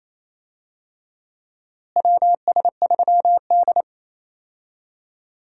Morse code, heard as a keyed tone, about 27 wpm, 700 Hz. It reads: WS3D
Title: USA Call Signs - 4 Characters - Repeated 2x - 28wpm